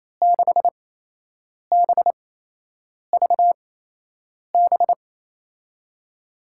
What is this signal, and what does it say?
Morse code, heard as a keyed tone, about 28 wpm, spaced as for 8 wpm, 700 Hz. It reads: 6BVB
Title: USA Call Signs - 6 Characters - 8x Character Spacing - 28wpm